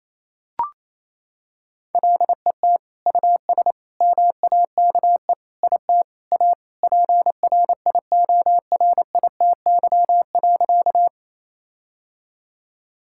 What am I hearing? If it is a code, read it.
Morse code, heard as a keyed tone, about 27 wpm, 700 Hz. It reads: LET US MAKE IT A PRIORITY.
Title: Sentences from Top 1600 Words 28wpm